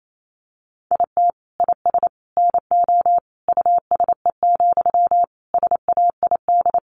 Morse code, heard as a keyed tone, about 28 wpm, 700 Hz. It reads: IT IS NO USE, SAID
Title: Sentences from Top 500 Words 28wpm